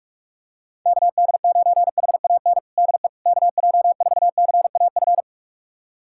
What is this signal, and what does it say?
Morse code, heard as a keyed tone, about 43 wpm, 690 Hz. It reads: KD9HAN DE KJ4CAF
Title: USA Callsign DE Callsign 45wpm